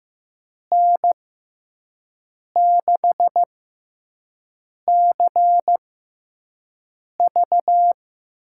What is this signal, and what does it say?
Morse code, heard as a keyed tone, about 15 wpm, spaced as for 5 wpm, 700 Hz. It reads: N6CV
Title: USA Call Signs - 6 Characters - 6x Character Spacing - 15wpm